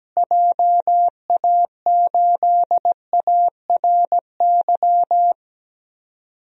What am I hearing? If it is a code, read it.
Morse code, heard as a keyed tone, about 17 wpm, 700 Hz. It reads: JA8ARY
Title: DXCC-100 Call Signs - 6 Characters - Repeated 2x - 17wpm